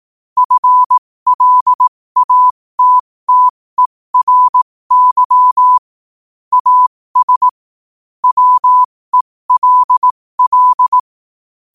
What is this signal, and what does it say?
Morse code, heard as a keyed tone, about 18 wpm, spaced as for 15 wpm, 1000 Hz. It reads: FLATTERY AS WELL
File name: qotdfiles / 15wpm